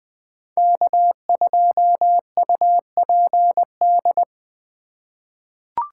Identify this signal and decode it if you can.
Morse code, heard as a keyed tone, about 21 wpm, 700 Hz. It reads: K2UPD
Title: 2023 POTA Hunter Call Signs - 20wpm